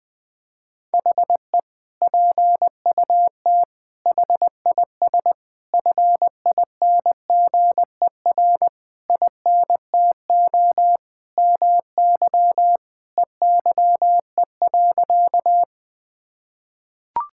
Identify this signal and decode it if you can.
Morse code, heard as a keyed tone, about 20 wpm, 700 Hz. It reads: HE PUT HIS FINGER INTO MY EYE.
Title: Sentences from Top 1600 Words 20wpm